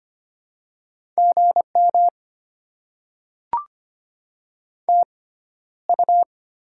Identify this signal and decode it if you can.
Morse code, heard as a keyed tone, about 25 wpm, spaced as for 13 wpm, 700 Hz. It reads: GM TU
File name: Top 20 QSO Elements - 6x Character Spacing - 25wpm